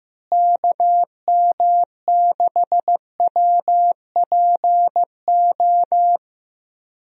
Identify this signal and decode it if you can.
Morse code, heard as a keyed tone, about 15 wpm, 700 Hz. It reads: KM6WPO
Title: USA Call Signs - 6 Characters - Repeated 2x - 15wpm